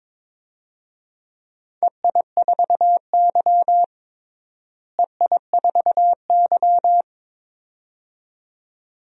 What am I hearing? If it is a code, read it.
Morse code, heard as a keyed tone, about 22 wpm, 700 Hz. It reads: EI4Y EI4Y
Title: DXCC-100 Call Signs - 4 Characters - Repeated 2x - 22wpm